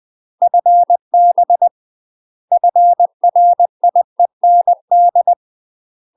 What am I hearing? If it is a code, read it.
Morse code, heard as a keyed tone, about 20 wpm, 700 Hz. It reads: FB FRIEND
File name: UK MASTER QSOs TRAINING / 20WPM / UK MASTER QSOs 0060